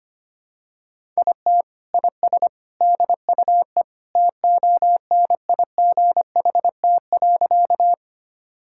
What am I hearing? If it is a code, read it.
Morse code, heard as a keyed tone, about 25 wpm, 700 Hz. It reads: IT IS DUE TONIGHT.
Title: Sentences from Top 1000 Words - Review Part 6 - 25wpm